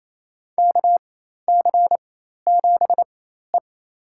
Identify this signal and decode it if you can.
Morse code, heard as a keyed tone, about 27 wpm, spaced as for 13 wpm, 700 Hz. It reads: KC7E
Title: USA Call Signs - 4x Character Spacing - 28wpm